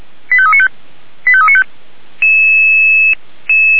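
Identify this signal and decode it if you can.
Five-tone selective call (ZVEI), heard as quick runs of five tones, tones of about 80 ms. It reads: 85395 85395
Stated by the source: Five-tone ZVEI selective call received on SDR by Feuerwehr Orlamünde.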